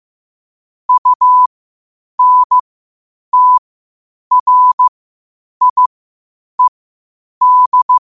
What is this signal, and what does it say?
Morse code, heard as a keyed tone, about 15 wpm, spaced as for 8 wpm, 1000 Hz. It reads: UNTRIED
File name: qotdfiles / 8wpm